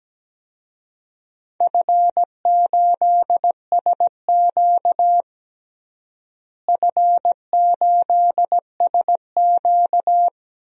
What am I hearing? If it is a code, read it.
Morse code, heard as a keyed tone, about 17 wpm, 700 Hz. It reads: F8SQ F8SQ
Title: DXCC-100 Call Signs - 4 Characters - Repeated 2x - 17wpm